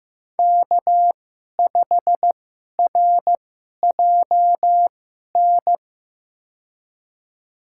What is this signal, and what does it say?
Morse code, heard as a keyed tone, about 15 wpm, spaced as for 11 wpm, 700 Hz. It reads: K5RJN
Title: USA Call Signs - 5 Characters - 2x Character Spacing - 15wpm